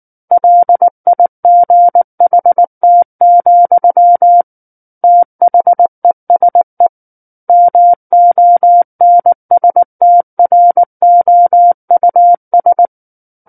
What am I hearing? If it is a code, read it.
Morse code, heard as a keyed tone, about 19 wpm, 700 Hz. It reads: LIGHT, THESE MONSTROUS
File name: Book One / 10-Chapter 10 - 19 WPM